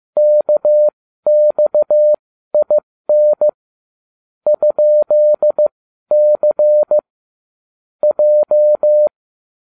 Morse code, heard as a keyed tone, about 15 wpm, spaced as for 12 wpm, 600 Hz. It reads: KXIN ?C J